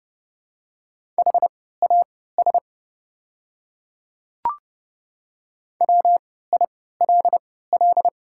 Morse code, heard as a keyed tone, about 30 wpm, spaced as for 17 wpm, 700 Hz. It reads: HAS WILL
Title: 3x Character Spacing - Top 100 Words - 30wpm